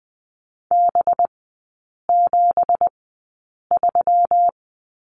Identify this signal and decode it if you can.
Morse code, heard as a keyed tone, about 20 wpm, spaced as for 8 wpm, 690 Hz. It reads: B73
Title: 198 Lesson 32e - Letter k - Three Characters 20wpm